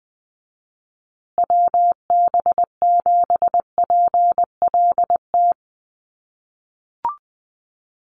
Morse code, heard as a keyed tone, about 19 wpm, 700 Hz. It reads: WB7PLT
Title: USA Call Signs Repeated 3x 20wpm